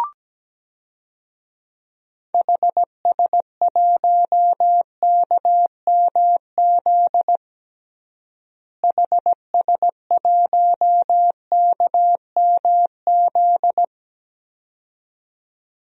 Morse code, heard as a keyed tone, about 17 wpm, 700 Hz. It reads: HS1KMZ HS1KMZ
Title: DXCC-100 Call Signs - 6 Characters - Repeated 2x - 17wpm